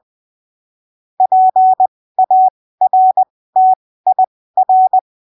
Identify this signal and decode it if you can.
Morse code, heard as a keyed tone, about 20 wpm, spaced as for 15 wpm, 750 Hz. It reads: PARTIR